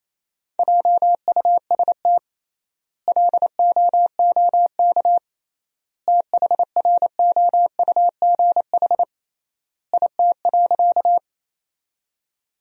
Morse code, heard as a keyed tone, about 28 wpm, 700 Hz. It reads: JUST LOOK THROUGH IT.
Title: 3x Word-Spacing - Sentences from Top 200 Words - 28wpm